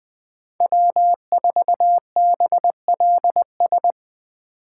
Morse code, heard as a keyed tone, about 20 wpm, 700 Hz. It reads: W4BLS